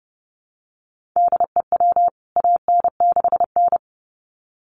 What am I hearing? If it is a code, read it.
Morse code, heard as a keyed tone, about 30 wpm, 700 Hz. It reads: DEW AN6N
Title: CWT Exchange 30wpm